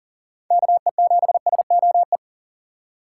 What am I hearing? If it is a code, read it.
Morse code, heard as a keyed tone, about 40 wpm, 700 Hz. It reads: KE7SOE
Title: USA Call Signs - 6x Character Spacing - 40wpm